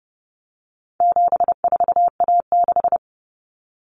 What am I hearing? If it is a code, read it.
Morse code, heard as a keyed tone, about 30 wpm, 700 Hz. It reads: 74A6
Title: Common Cut Numbers - 4-Digit 30wpm